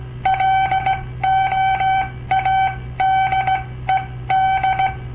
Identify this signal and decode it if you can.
Morse code, heard as a keyed tone, about 17 wpm, spaced as for 14 wpm, 770 Hz. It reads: LOADED
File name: papertapes / local / Code Practice-01